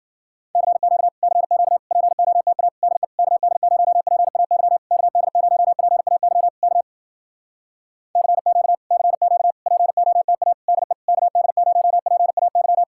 Morse code, heard as a keyed tone, about 59 wpm, 700 Hz. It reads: CQ CQ POTA DE KD0PAY KD0PAY K CQ CQ POTA DE KD0PAY